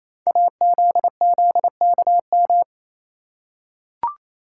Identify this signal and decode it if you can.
Morse code, heard as a keyed tone, about 29 wpm, 700 Hz. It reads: AZZKM
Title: Code Groups - 5 Letters - 28wpm